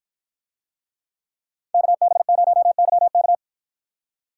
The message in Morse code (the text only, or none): KB0YK